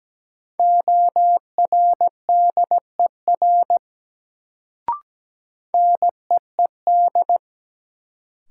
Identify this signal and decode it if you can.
Morse code, heard as a keyed tone, about 17 wpm, 700 Hz. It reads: ORDER NEED
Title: Sets of 2 Words - New Word Drill - 17wpm